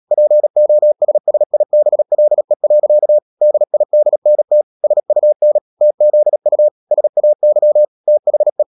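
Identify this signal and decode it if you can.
Morse code, heard as a keyed tone, about 37 wpm, 600 Hz. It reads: POSSIBLE. DIDNT SUN TZU SAY THE
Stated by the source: Morse code practice at 35–39 wpm sending English text of Ken Follett's novel Never, from CW med Gnister.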